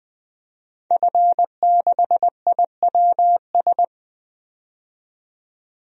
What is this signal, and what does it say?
Morse code, heard as a keyed tone, about 20 wpm, 700 Hz. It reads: F6IWS
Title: DXCC-100 Call Signs - 5 Characters - 20wpm